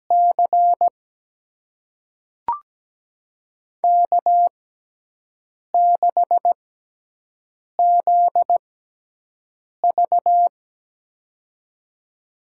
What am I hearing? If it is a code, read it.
Morse code, heard as a keyed tone, about 17 wpm, spaced as for 6 wpm, 700 Hz. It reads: C K6ZV
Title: USA Call Signs - 4 Characters - 6x Character Spacing - 17wpm